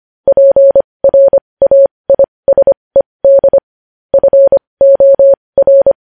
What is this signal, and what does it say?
Morse code, heard as a keyed tone, about 25 wpm, spaced as for 20 wpm, 550 Hz. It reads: PRAISED FOR
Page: news-headlines-in-morse-code-at-20-wpm